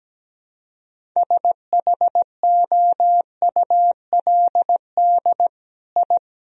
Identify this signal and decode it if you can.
Morse code, heard as a keyed tone, about 17 wpm, 700 Hz. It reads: SHOULD I
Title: Sentences from Top 1000 Words - Review Part 2 - 17wpm